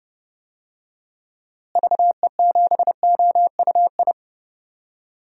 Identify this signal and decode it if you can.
Morse code, heard as a keyed tone, about 32 wpm, 700 Hz. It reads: VE7OUI
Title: DXCC-100 Call Signs - 6 Characters - Repeated 3x - 30wpm